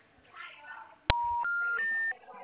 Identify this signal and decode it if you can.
Special information tone, with speech.